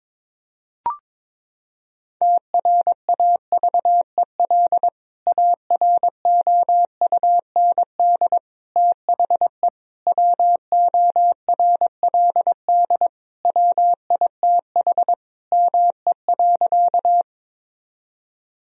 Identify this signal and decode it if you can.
Morse code, heard as a keyed tone, about 21 wpm, 700 Hz. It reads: TRAVEL AROUND THE WORLD WITH ME.
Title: Sentences from Top 1000 Words - Review Part 3 - 22wpm